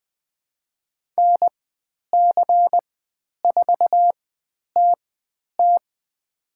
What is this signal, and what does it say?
Morse code, heard as a keyed tone, about 20 wpm, spaced as for 10 wpm, 700 Hz. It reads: NC4TT
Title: 095 Lesson 16g - Letter c - Call Signs 20wpm